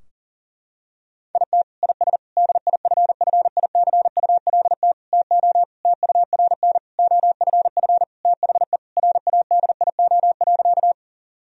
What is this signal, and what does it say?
Morse code, heard as a keyed tone, about 40 wpm, 700 Hz. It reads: IT IS DIFFICULT TO TURN OFF THE RADIO.
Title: Sentences from Top 1000 Words - Review Part 3 - 40wpm